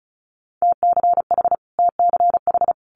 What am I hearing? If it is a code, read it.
Morse code, heard as a keyed tone, about 34 wpm, 700 Hz. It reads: TCH TCH
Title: Multi-letter Phonemes Repeated 3x 35wpm